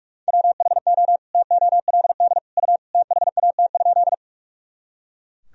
Morse code, heard as a keyed tone, about 42 wpm, 700 Hz. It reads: WHO TOLD U THAT?
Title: Sentences from Top 400 Words and Abbreviations 45wpm